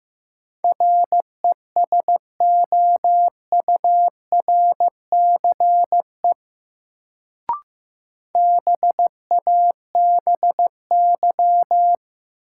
Morse code, heard as a keyed tone, about 15 wpm, 700 Hz. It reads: RESOURCE BABY
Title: Sentences from Top 1300 Words - New Word Drill Alt - 15wpm